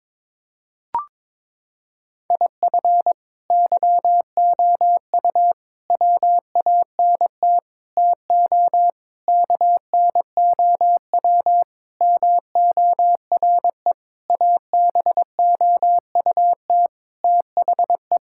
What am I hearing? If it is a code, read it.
Morse code, heard as a keyed tone, about 22 wpm, 700 Hz. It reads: IF YOU WANT TO KNOW MORE ABOUT THE
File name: practice / 22wpm / Sentences with 8-Word Phrases - 22wpm